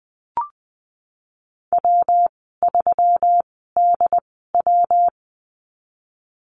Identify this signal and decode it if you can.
Morse code, heard as a keyed tone, about 20 wpm, spaced as for 14 wpm, 700 Hz. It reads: W3DW
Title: 128 Lesson 21g - Number 6 - Call Signs 20wpm